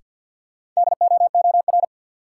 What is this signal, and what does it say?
Morse code, heard as a keyed tone, about 47 wpm, 700 Hz. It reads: DOOR
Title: Sentences from Top 300 Words - New Word Drill - 50wpm